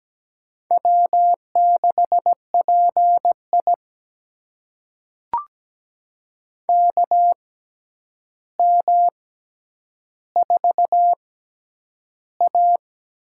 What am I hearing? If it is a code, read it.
Morse code, heard as a keyed tone, about 17 wpm, spaced as for 9 wpm, 700 Hz. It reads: W6PI KM4A
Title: USA Call Signs - 6x Character Spacing - 17wpm